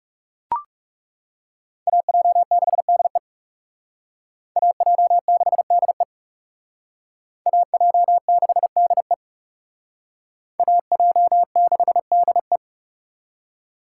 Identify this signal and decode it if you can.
Morse code, heard as a keyed tone, about 37 wpm, 700 Hz. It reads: AJ6DE AJ6DE AJ6DE AJ6DE
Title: Speed Racing - USA Call Signs 30wpm